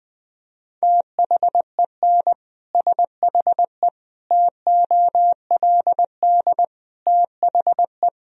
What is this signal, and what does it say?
Morse code, heard as a keyed tone, about 21 wpm, 700 Hz. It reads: THEN SHE TOLD THE
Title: Sentences from Top 1000 Words - Review Part 2 - 20wpm